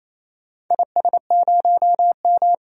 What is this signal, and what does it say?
Morse code, heard as a keyed tone, about 28 wpm, 700 Hz. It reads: IS0M
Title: DXCC-100 Call Signs - 4 Characters - 28wpm